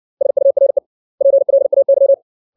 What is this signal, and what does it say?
Morse code, heard as a keyed tone, about 60 wpm, 550 Hz. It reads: SURE PLAY
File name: 632 words / 632 Words 60WPM